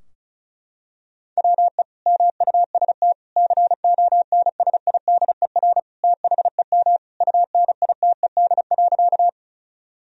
Morse code, heard as a keyed tone, about 36 wpm, 700 Hz. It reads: WE MUST CONSIDER THEM UNITED.
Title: Sentences from Top 1000 Words - Review Part 4 - 35wpm